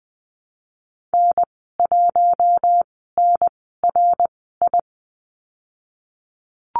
Morse code, heard as a keyed tone, about 20 wpm, spaced as for 15 wpm, 700 Hz. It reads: N1NRI
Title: 055 Lesson 10g - Letter r - Call Signs 20wpm